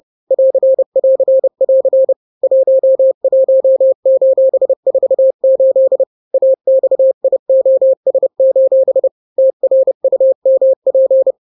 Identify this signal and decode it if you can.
Morse code, heard as a keyed tone, about 30 wpm, 530 Hz. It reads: <AR><AR><AR> 11:48 AXIOS: TRUMP
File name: daily / 29wpm